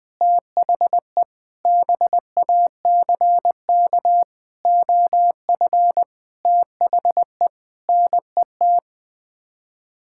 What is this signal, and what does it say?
Morse code, heard as a keyed tone, about 20 wpm, 700 Hz. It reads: THE BACK OF THE NET
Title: Sets of 8 Words - 20wpm